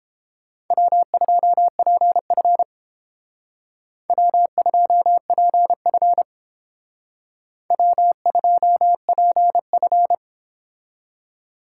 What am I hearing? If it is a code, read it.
Morse code, heard as a keyed tone, about 28 wpm, 700 Hz. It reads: W2PF W2PF W2PF